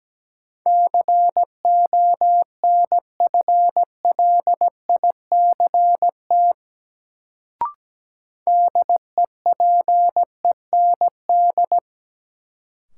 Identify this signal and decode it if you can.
Morse code, heard as a keyed tone, about 17 wpm, 700 Hz. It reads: CONFLICT DEPEND
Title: Sentences from Top 1300 Words - New Word Drill Alt - 17wpm